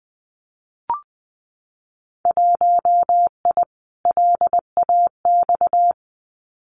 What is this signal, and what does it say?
Morse code, heard as a keyed tone, about 20 wpm, 700 Hz. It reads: 1I LAX
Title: Winter Field Day Exchanges 20wpm